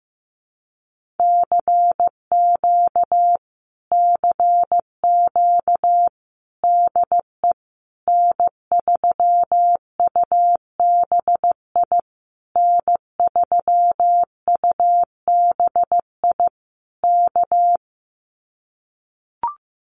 Morse code, heard as a keyed tone, about 15 wpm, 700 Hz. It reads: CQ CQ DE N3UBI N3UBI K
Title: Common QSO Phrases Encore 15wpm